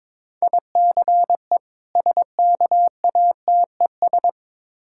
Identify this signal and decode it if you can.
Morse code, heard as a keyed tone, about 22 wpm, 690 Hz. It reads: ICE SKATES
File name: Difficult Words with Hints - 22wpm